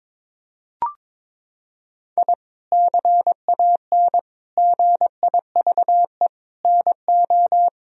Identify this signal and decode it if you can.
Morse code, heard as a keyed tone, about 21 wpm, 700 Hz. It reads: I CAN GIVE NO
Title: Sentences from Top 1000 Words - Review Part 3 - 22wpm